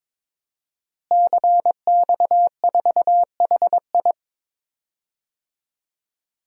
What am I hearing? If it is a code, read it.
Morse code, heard as a keyed tone, about 22 wpm, 700 Hz. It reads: CX4HI
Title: DXCC-100 Call Signs - 5 Characters - Repeated 2x - 22wpm